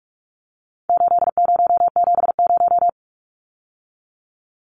Morse code, heard as a keyed tone, about 44 wpm, 700 Hz. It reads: 8070
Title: Numbers - 4-Digit 45wpm